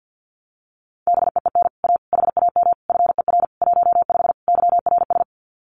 Morse code, heard as a keyed tone, about 50 wpm, 700 Hz. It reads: BEEN A HAM FER 15 YRS
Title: Common QSO Phrases Encore 50wpm